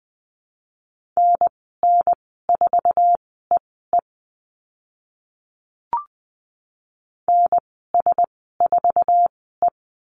Morse code, heard as a keyed tone, about 20 wpm, spaced as for 14 wpm, 700 Hz. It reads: NN4EE NS4E
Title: 088 Lesson 15g - Number 5 - Call Signs 20wpm